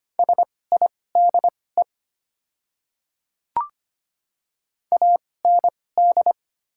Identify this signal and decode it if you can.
Morse code, heard as a keyed tone, about 25 wpm, spaced as for 18 wpm, 700 Hz. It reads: SIDE AND
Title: 2x Character Spacing - Top 200 Words - 25wpm